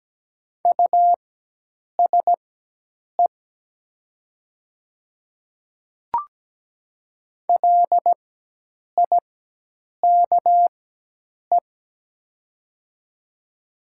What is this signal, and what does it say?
Morse code, heard as a keyed tone, about 17 wpm, spaced as for 8 wpm, 700 Hz. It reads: USE LIKE
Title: 4x Character Spacing - Top 100 Words - 17wpm